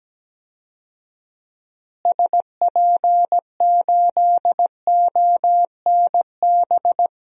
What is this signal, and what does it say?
Morse code, heard as a keyed tone, about 17 wpm, 700 Hz. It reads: SP8ONB
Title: DXCC-100 Call Signs - 6 Characters - Repeated 3x - 17wpm